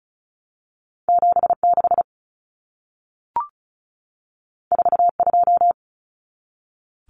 Morse code, heard as a keyed tone, about 36 wpm, 700 Hz. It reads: 76 42
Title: Numbers - 2-Digit 35wpm